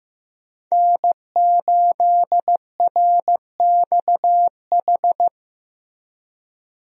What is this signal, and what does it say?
Morse code, heard as a keyed tone, about 16 wpm, 700 Hz. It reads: N8RXH
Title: USA Call Signs - 5 Characters - Repeated 3x - 15wpm